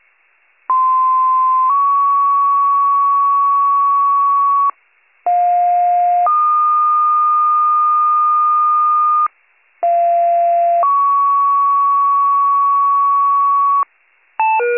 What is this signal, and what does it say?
Two-tone page, heard as a one-second tone, then a three-second tone, about 1040 then 1120 Hz, about 710 then 1220 Hz, about 690 then 1120 Hz. It is heard from 0.7 to 13.8 s.